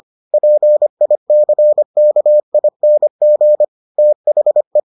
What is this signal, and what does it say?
Morse code, heard as a keyed tone, about 25 wpm, 600 Hz. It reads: PICKING THE